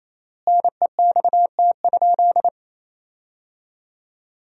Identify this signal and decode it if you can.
Morse code, heard as a keyed tone, about 27 wpm, 700 Hz. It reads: NEXT?
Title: 4x Word-Spacing - Sentences from Top 300 Words - 28wpm